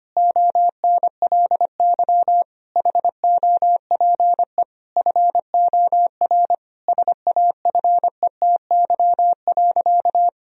Morse code, heard as a keyed tone, about 25 wpm, 700 Hz. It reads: ONLY HOPE FOR SAFETY.
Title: Sentences from Top 1000 Words - Review Part 4 - 25wpm